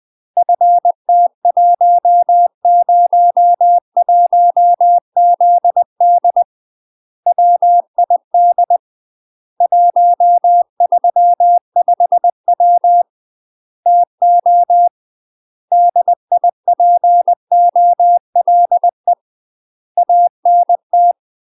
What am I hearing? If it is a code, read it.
Morse code, heard as a keyed tone, about 20 wpm, 700 Hz. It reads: FT101ZD WID 135W TO DIPOLE ANT
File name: 20WPM